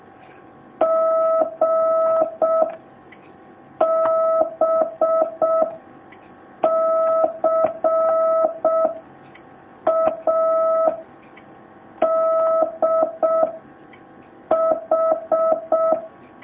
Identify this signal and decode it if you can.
Morse code, heard as a keyed tone, about 6 wpm, 650 Hz. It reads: G B C A D H
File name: jh8cbh / 2019cw